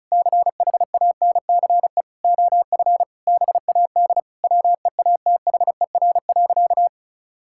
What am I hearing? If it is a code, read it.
Morse code, heard as a keyed tone, about 36 wpm, 690 Hz. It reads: CHANCE OF BAD WEATHER.
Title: Sentences from Top 1000 Words - Review Part 6 - 35wpm